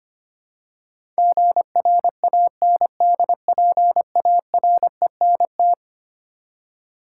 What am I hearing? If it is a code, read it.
Morse code, heard as a keyed tone, about 25 wpm, 700 Hz. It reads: GRANDPARENT
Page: morse-code-ninja-practice-25wpm